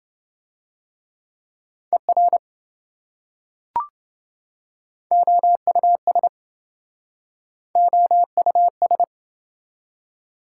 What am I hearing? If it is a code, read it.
Morse code, heard as a keyed tone, about 27 wpm, 700 Hz. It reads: ER OUS OUS